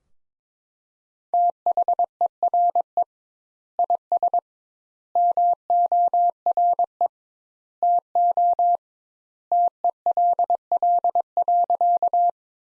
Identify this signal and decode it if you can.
Morse code, heard as a keyed tone, about 22 wpm, 700 Hz. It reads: THERE IS MORE TO TELL.
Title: Sentences Top 200 Words - 22wpm